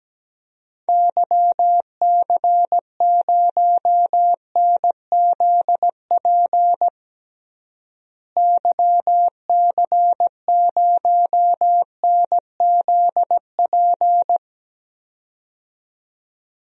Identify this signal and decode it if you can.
Morse code, heard as a keyed tone, about 17 wpm, 700 Hz. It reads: YC0NZP YC0NZP
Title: DXCC-100 Call Signs - 6 Characters - Repeated 2x - 17wpm